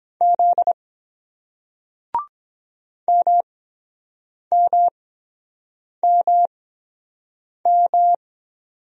Morse code, heard as a keyed tone, about 24 wpm, spaced as for 7 wpm, 700 Hz. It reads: Z MMMM